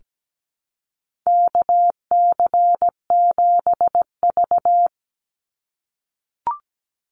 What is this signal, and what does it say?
Morse code, heard as a keyed tone, about 17 wpm, 700 Hz. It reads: KC7V
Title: Call Signs - 17wpm